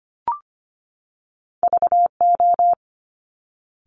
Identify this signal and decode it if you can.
Morse code, heard as a keyed tone, about 25 wpm, 700 Hz. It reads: VO
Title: Top Two Letter Combinations in Words 25wpm